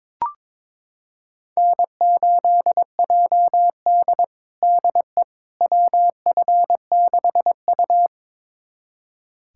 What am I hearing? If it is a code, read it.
Morse code, heard as a keyed tone, about 22 wpm, 700 Hz. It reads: N8JD DE WF6U
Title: USA Callsign DE Callsign - 4 Characters - 22wpm